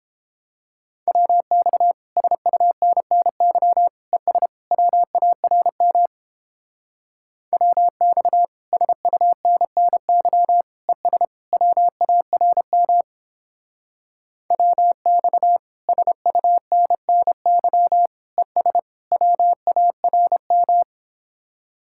Morse code, heard as a keyed tone, about 29 wpm, 700 Hz. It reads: WX SUNNY ES WARM WX SUNNY ES WARM WX SUNNY ES WARM